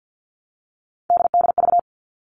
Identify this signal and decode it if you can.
Morse code, heard as a keyed tone, about 50 wpm, 700 Hz. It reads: DDV
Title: All Cut Numbers - 3-Digit 50wpm